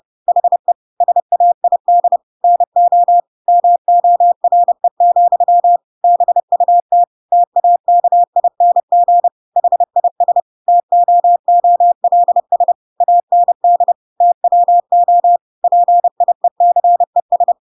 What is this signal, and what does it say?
Morse code, heard as a keyed tone, about 30 wpm, 700 Hz. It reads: HE SAID NO MORE, BUT TAKING HIS TOOLS AND TWO PIECES
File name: Pinocchio / Pinocchio ch8 30 30 ik1hge